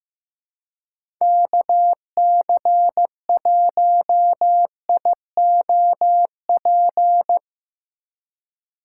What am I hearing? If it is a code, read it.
Morse code, heard as a keyed tone, about 15 wpm, 700 Hz. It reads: KC1IOP
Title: USA Call Signs - 6 Characters - 3x Character Spacing - 15wpm